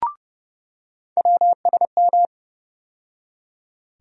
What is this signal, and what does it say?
Morse code, heard as a keyed tone, about 31 wpm, 700 Hz. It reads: WSM
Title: USA Call Sign Postfixes - 3 Letters 30wpm